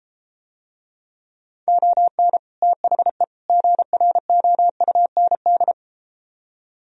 Morse code, heard as a keyed tone, about 33 wpm, 700 Hz. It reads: ON THE GROUND